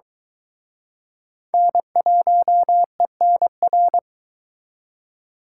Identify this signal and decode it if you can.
Morse code, heard as a keyed tone, about 23 wpm, 700 Hz. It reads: N1ENR